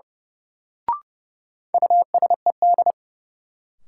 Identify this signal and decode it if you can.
Morse code, heard as a keyed tone, about 30 wpm, 700 Hz. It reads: USED
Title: Sentences from Top 600 Words - New Word Drill - 30wpm